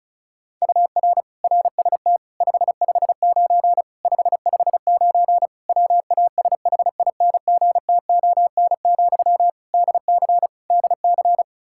Morse code, heard as a keyed tone, about 36 wpm, 700 Hz. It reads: UR RST 559 559 WASHINGTON, DC DC
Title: Practice QSO - Exchange 1 - Beginning - 35wpm